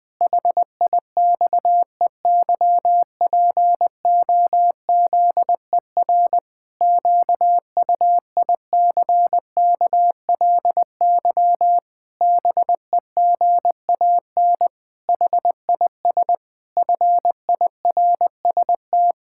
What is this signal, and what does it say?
Morse code, heard as a keyed tone, about 20 wpm, 700 Hz. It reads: HIXEYPOZER QUICKLY BEGAN HIS FIRST
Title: Pangram Challenge v2 - 20wpm